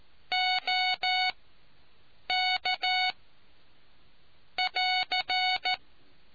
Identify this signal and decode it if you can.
Morse code, heard as a keyed tone, about 14 wpm, spaced as for 6 wpm, 740 Hz. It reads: OK<AR>